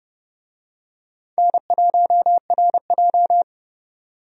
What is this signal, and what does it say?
Morse code, heard as a keyed tone, about 30 wpm, 700 Hz. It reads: N1RJ
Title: USA Call Signs - 4 Characters - 30wpm